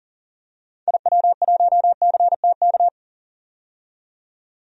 Morse code, heard as a keyed tone, about 38 wpm, 700 Hz. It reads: IW1CTK
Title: DXCC-100 Call Signs - 6 Characters - 40wpm